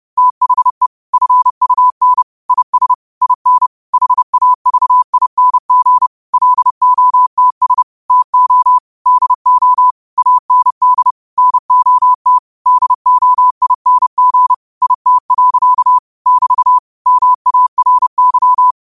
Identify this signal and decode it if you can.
Morse code, heard as a keyed tone, about 30 wpm, 1000 Hz. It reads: THE FUN IS IN HAVING LOTS TO DO AND NOT DOING IT. <BT> MARY
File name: qotdfiles / 30wpm